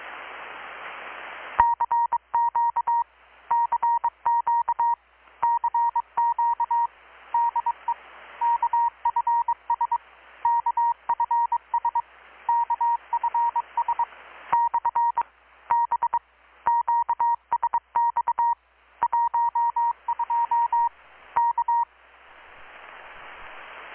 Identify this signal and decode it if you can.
Morse code, heard as a keyed tone, about 21 wpm, 940 Hz. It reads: CQ CQ CQ DE KFS KFS KFS / B QSX 12 K
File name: blog / swl / 6-abril-2019-12695-USB-KPS